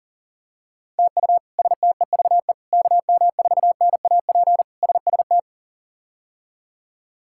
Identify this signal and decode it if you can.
Morse code, heard as a keyed tone, about 40 wpm, 700 Hz. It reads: TU STEVE KM4NAP SST
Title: K1USN SST QSO Practice - 40wpm